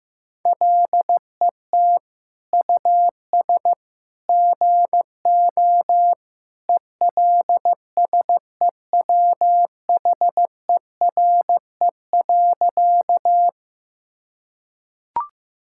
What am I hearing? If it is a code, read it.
Morse code, heard as a keyed tone, about 15 wpm, 700 Hz. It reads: LET US GO ELSEWHERE.
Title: Sentences from Top 1600 Words 15wpm